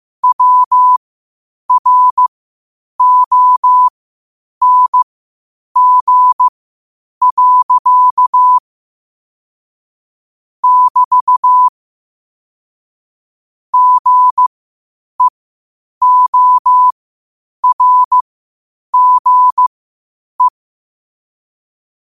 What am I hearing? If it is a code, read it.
Morse code, heard as a keyed tone, about 15 wpm, spaced as for 8 wpm, 1000 Hz. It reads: WRONG. <BT> GEORGE
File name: qotdfiles / 8wpm